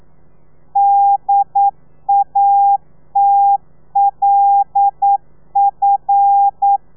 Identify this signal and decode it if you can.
Morse code, heard as a keyed tone, about 9 wpm, 800 Hz. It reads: DATLF
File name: telegrafia / i2tfj / Lez5 40